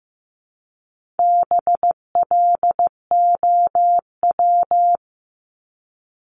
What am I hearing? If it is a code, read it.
Morse code, heard as a keyed tone, about 15 wpm, 700 Hz. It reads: BLOW
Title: Top 1751-2000 Words 15wpm